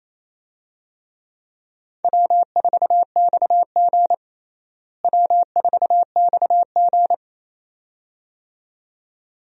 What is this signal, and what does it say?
Morse code, heard as a keyed tone, about 28 wpm, 700 Hz. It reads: W4XG W4XG
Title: USA Call Signs - 4 Characters - Repeated 2x - 28wpm